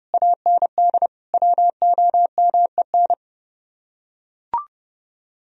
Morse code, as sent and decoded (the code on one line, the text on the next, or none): .- -. -.. / .-- --- -- . -.
AND WOMEN